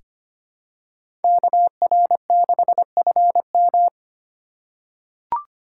Morse code, heard as a keyed tone, about 25 wpm, 700 Hz. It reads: KR6FM
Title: USA Call Signs - 5 Characters - 2x Character Spacing - 25wpm